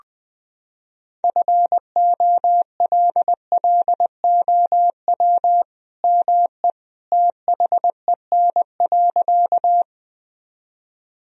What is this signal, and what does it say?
Morse code, heard as a keyed tone, about 20 wpm, 700 Hz. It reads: FOLLOW ME THEN.
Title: Sentences from Top 1000 Words - Review Part 1 - 20wpm